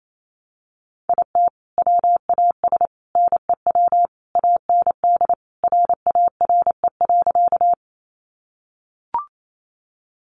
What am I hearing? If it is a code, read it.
Morse code, heard as a keyed tone, about 28 wpm, 700 Hz. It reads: IT WAS NEW AND RARE.
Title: Sentences from Top 1100 Words 28wpm